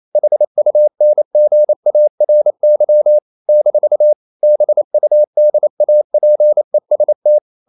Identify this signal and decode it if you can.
Morse code, heard as a keyed tone, about 28 wpm, 600 Hz. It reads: HUNGARY - BUDAPEST
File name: CW / Practice / NationalCapitals28